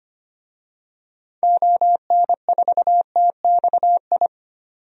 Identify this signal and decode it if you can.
Morse code, heard as a keyed tone, about 25 wpm, 700 Hz. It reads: ON4TXI
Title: DXCC-100 Call Signs - 6 Characters - 25wpm